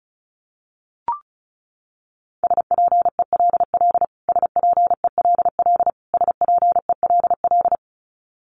Morse code, heard as a keyed tone, about 34 wpm, 700 Hz. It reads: SPELL SPELL SPELL
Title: Top 200 Words Repeated 35wpm